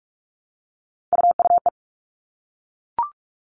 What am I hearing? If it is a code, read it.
Morse code, heard as a keyed tone, about 44 wpm, 700 Hz. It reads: UUE
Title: All Cut Numbers - 3-Digit 45wpm